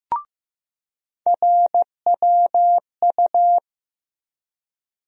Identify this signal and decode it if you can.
Morse code, heard as a keyed tone, about 15 wpm, 700 Hz. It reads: RWU
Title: USA Call Sign Postfixes - 3 Letters 15wpm